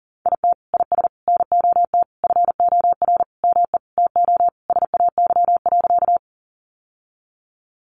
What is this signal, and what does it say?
Morse code, heard as a keyed tone, about 40 wpm, 700 Hz. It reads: IT IS NOT FOR ME TO SAY.
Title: Sentences from Top 600 Words 40wpm